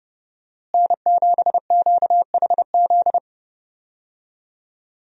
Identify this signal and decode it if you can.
Morse code, heard as a keyed tone, about 30 wpm, 700 Hz. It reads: N7QHZ
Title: USA Call Signs - 5 Characters - Repeated 2x - 30wpm